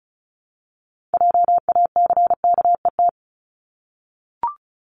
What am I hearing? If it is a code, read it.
Morse code, heard as a keyed tone, about 34 wpm, 700 Hz. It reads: JACKET
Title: Top 1751-2000 Words 35wpm